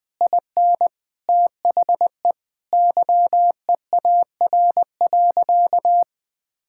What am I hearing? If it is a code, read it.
Morse code, heard as a keyed tone, about 20 wpm, 700 Hz. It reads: IN THE YEAR.
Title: Sentences from Top 1000 Words - Review Part 5 - 20wpm